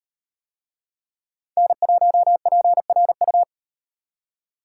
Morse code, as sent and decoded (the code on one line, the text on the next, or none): -. .---- .--. .-. ..-
N1PRU